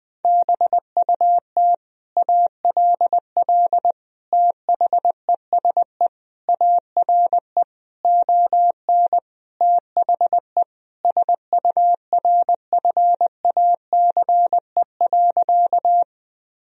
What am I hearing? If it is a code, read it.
Morse code, heard as a keyed tone, about 20 wpm, 700 Hz. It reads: BUT ALL THESE ARE ON THE SURFACE.
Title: Sentences from Top 1000 Words - Review Part 3 - 20wpm